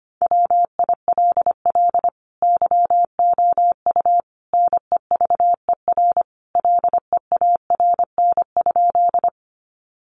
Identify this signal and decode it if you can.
Morse code, heard as a keyed tone, about 25 wpm, 700 Hz. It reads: WILL YOU NEVER LEARN?
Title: Sentences from Top 300 Words 25wpm